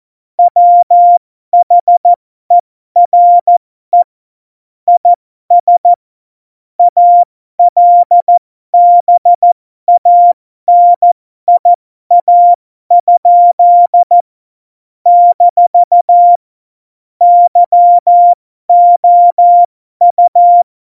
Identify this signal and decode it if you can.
Morse code, heard as a keyed tone, about 14 wpm, spaced as for 12 wpm, 700 Hz. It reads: WHERE IS ALBANIA? - YOU
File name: Morsekurs / Witze